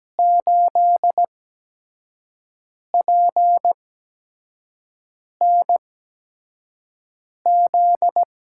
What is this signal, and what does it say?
Morse code, heard as a keyed tone, about 17 wpm, spaced as for 5 wpm, 700 Hz. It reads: 8PNZ
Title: USA Call Signs - 5 Characters - 8x Character Spacing - 17wpm